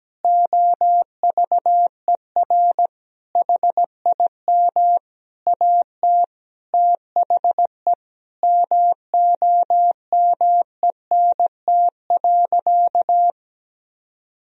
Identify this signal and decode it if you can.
Morse code, heard as a keyed tone, about 17 wpm, 700 Hz. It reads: OVER HIM AT THE MOMENT.
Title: Sentences from Top 1000 Words - Review Part 3 - 17wpm